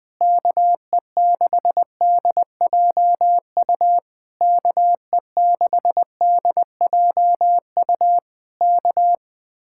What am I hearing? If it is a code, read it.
Morse code, heard as a keyed tone, about 21 wpm, 700 Hz. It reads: KE6DJU KE6DJU K